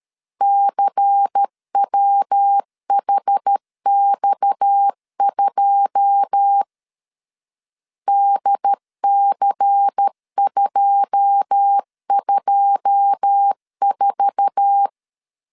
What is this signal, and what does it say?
Morse code, heard as a keyed tone, about 13 wpm, 800 Hz. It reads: CWHX2 DC224
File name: LEZIONI / 9 2 12wpm 2301